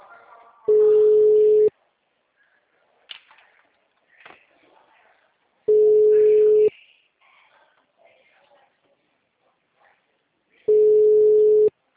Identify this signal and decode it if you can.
Ringback tone (Europe, 425 Hz), with speech.